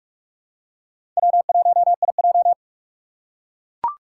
Morse code, heard as a keyed tone, about 42 wpm, 700 Hz. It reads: W1IJ
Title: USA Call Signs - 4 Characters - 3x Character Spacing - 45wpm